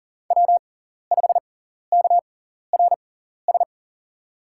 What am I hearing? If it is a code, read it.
Morse code, heard as a keyed tone, about 37 wpm, spaced as for 14 wpm, 700 Hz. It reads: W5KRS
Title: USA Call Signs - 6x Character Spacing - 40wpm